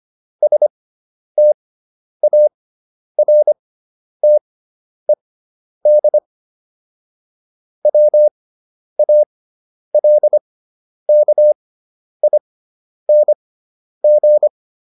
Morse code, heard as a keyed tone, about 25 wpm, spaced as for 10 wpm, 600 Hz. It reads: STARTED WALKING